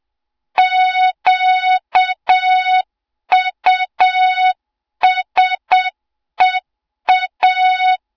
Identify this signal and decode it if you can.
Morse code, heard as a keyed tone, about 7 wpm, 3700 Hz, with noise. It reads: QUSEA
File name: Morze-HA2MN / hang / 20 GYAKORLAT